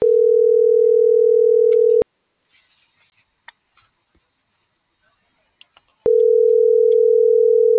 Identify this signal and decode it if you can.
Ringback tone (North America), 440 + 480 Hz.